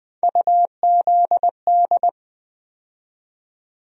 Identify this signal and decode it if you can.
Morse code, heard as a keyed tone, about 20 wpm, 700 Hz. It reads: UZD
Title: USA Call Sign Postfixes - 3 Letters 20wpm